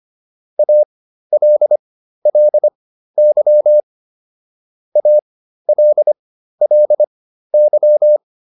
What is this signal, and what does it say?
Morse code, heard as a keyed tone, about 25 wpm, spaced as for 13 wpm, 600 Hz. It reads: ALLY ALLY